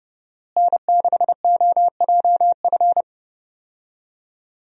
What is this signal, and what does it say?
Morse code, heard as a keyed tone, about 30 wpm, 700 Hz. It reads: N6OJF
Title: USA Call Signs - 5 Characters - Repeated 2x - 30wpm